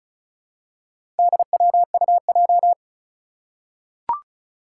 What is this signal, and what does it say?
Morse code, heard as a keyed tone, about 36 wpm, 700 Hz. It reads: DWUJ
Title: Four Challenging Letter-Number Combinations 35wpm